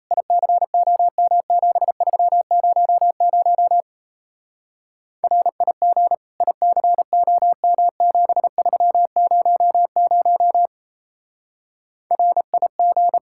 Morse code, heard as a keyed tone, about 35 wpm, 700 Hz. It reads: ICOM7300 RIG ICOM7300 RIG
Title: Speed-Racing Common QSO Phrases - 28wpm